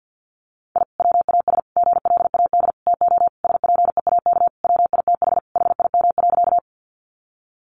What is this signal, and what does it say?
Morse code, heard as a keyed tone, about 48 wpm, 700 Hz. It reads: I WAS GLAD TO SPEAK WITH HIM.
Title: Sentences from Top 1100 Words 50wpm